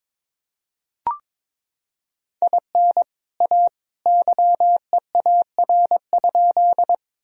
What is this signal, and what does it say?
Morse code, heard as a keyed tone, about 22 wpm, 700 Hz. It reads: IN A YEAR?
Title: Sentences from Top 1000 Words - Review Part 1 - 22wpm